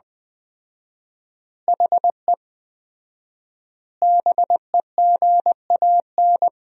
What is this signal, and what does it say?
Morse code, heard as a keyed tone, about 20 wpm, 700 Hz. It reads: HE BEGAN
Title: 4x Word-Spacing - Sentences from Top 300 Words - 20wpm